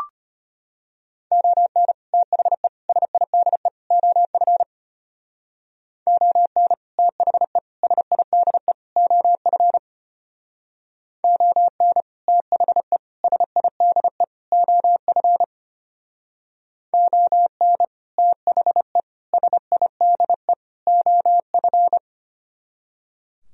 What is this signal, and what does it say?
Morse code, heard as a keyed tone, about 31 wpm, 700 Hz. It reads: ON THE SIDE OF ON THE SIDE OF ON THE SIDE OF ON THE SIDE OF